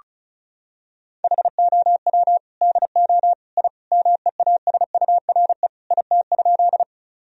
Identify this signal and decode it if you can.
Morse code, heard as a keyed tone, about 36 wpm, 700 Hz. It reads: HOW DO I MEASURE IT?
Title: Sentences from Top 1000 Words - Review Part 2 - 35wpm